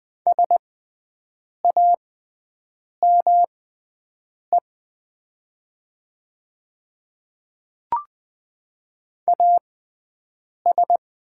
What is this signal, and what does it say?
Morse code, heard as a keyed tone, about 20 wpm, spaced as for 7 wpm, 700 Hz. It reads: SAME AS
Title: 6x Character Spacing - Top 200 Words - 20wpm